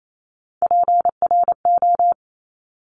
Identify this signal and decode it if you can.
Morse code, heard as a keyed tone, about 28 wpm, 700 Hz. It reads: PRO
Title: Top Three Letter Combinations in Words 28wpm